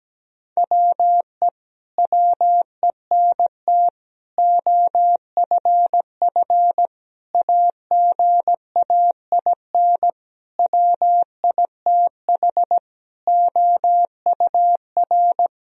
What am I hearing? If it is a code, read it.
Morse code, heard as a keyed tone, about 17 wpm, 700 Hz. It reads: WE WENT OFF AGAIN WITH OUR
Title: Sentences from Top 1000 Words - Review Part 3 - 17wpm